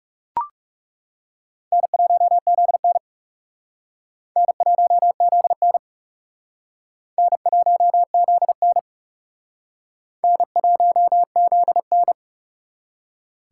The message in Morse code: N1ZN N1ZN N1ZN N1ZN

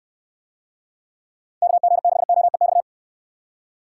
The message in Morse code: KG6OEX

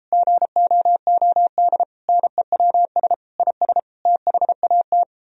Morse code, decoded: GOOD NEWS IS THAT

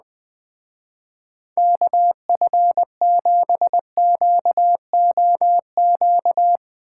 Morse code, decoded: KF7QOQ

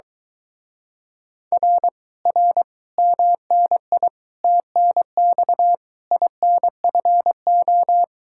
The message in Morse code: R R MNI TNX INFO

